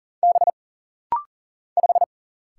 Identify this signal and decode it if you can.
Morse code, heard as a keyed tone, about 39 wpm, 700 Hz. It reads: B 5
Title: Single Letter-Number - Rapid-Fire 40wpm